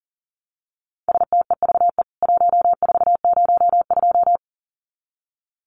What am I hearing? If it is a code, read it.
Morse code, heard as a keyed tone, about 40 wpm, 700 Hz. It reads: STEVE 1402
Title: CWT Exchange 40wpm